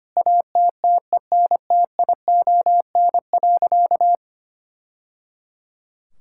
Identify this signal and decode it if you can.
Morse code, heard as a keyed tone, about 25 wpm, 700 Hz. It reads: ATTENTION.
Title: Sentences from Top 1000 Words - Review Part 6 - 25wpm